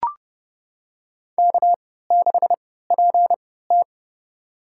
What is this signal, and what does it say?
Morse code, heard as a keyed tone, about 30 wpm, spaced as for 17 wpm, 700 Hz. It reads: K6PT
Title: USA Call Signs - 4 Characters - 3x Character Spacing - 30wpm